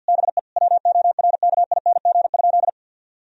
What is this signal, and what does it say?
Morse code, heard as a keyed tone, about 50 wpm, 700 Hz. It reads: BE WORKING?
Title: Sentences from Top 1000 Words - Review Part 4 - 50wpm